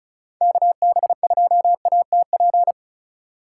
Morse code, heard as a keyed tone, about 34 wpm, 700 Hz. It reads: KB2ATP
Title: USA Call Signs - 2x Character Spacing - 35wpm